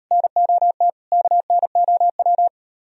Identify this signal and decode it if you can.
Morse code, heard as a keyed tone, about 38 wpm, 700 Hz. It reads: NOT KNOW